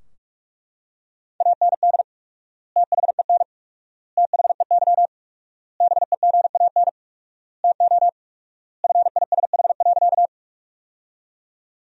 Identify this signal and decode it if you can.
Morse code, heard as a keyed tone, about 41 wpm, 700 Hz. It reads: AND THEN THEY BEGAN TO FISH.
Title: 4x Word-Spacing - Sentences from Top 300 Words - 45wpm